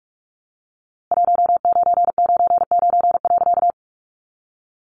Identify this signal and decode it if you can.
Morse code, heard as a keyed tone, about 44 wpm, 700 Hz. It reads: 1999.
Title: A Sound Of Thunder 45wpm